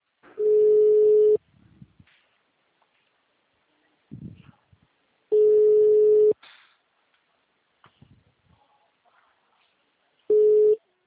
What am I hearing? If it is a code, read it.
Ringback tone (Europe, 425 Hz).
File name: out- -2010- - - .55403